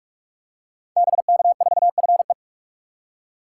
Morse code, heard as a keyed tone, about 44 wpm, 700 Hz. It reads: BK4FE
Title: Code Groups - 5 Letter:Numbers - 45wpm